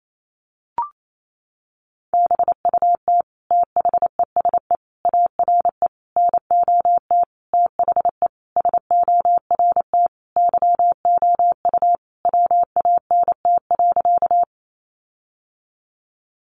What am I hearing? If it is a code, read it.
Morse code, heard as a keyed tone, about 28 wpm, 700 Hz. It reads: BUT THESE ARE NOT THE SORT YOU WANT.
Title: Sentences from Top 1300 Words 28wpm